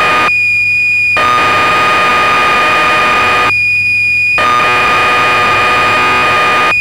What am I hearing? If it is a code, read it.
Cassette data (Kansas City type).
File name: magazines / Acorn User / 1984-11 BBC Programs